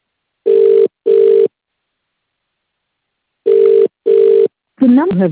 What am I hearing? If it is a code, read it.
Ringback tone (UK), 400 + 450 Hz.